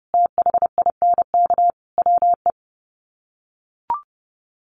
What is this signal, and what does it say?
Morse code, heard as a keyed tone, about 28 wpm, 700 Hz. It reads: THINK WE
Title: Sets of 5 Words 30wpm